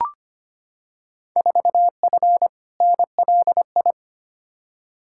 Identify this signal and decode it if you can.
Morse code, heard as a keyed tone, about 25 wpm, 700 Hz. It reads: 4F NLI
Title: Field Day Exchanges Encore - 25wpm